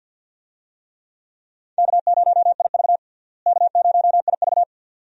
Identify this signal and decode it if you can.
Morse code, heard as a keyed tone, about 47 wpm, 700 Hz. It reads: K0IV K0IV
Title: USA Call Signs - 4 Characters - Repeated 2x - 50wpm